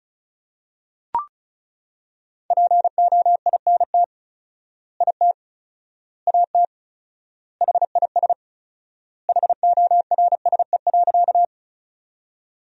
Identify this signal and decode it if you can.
Morse code, heard as a keyed tone, about 36 wpm, 700 Hz. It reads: POINT IT AT HIS HORSE.
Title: 4x Word-Spacing - Sentences from Top 300 Words - 35wpm